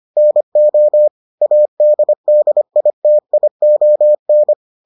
Morse code, heard as a keyed tone, about 25 wpm, 600 Hz. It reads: NO ADDITION